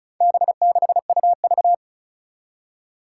Morse code, heard as a keyed tone, about 35 wpm, 700 Hz. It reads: B6UV
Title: Four Challenging Letter-Number Combinations 35wpm